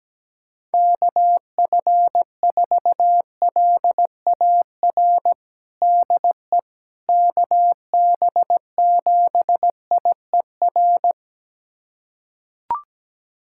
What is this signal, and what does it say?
Morse code, heard as a keyed tone, about 17 wpm, 700 Hz. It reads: KF4LAR DE KB7IER
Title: USA Callsign DE Callsign - 6 Characters - 17wpm